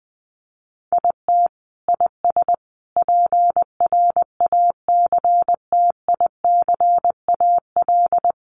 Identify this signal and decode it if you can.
Morse code, heard as a keyed tone, about 20 wpm, 700 Hz. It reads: IT IS PRACTICAL
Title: Sentences from Top 1400 Words 20wpm